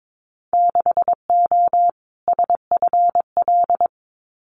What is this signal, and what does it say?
Morse code, heard as a keyed tone, about 22 wpm, 700 Hz. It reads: 6O SFL
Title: Winter Field Day Exchanges Encore 22wpm